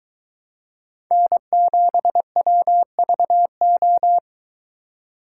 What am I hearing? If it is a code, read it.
Morse code, heard as a keyed tone, about 21 wpm, 700 Hz. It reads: N7WVO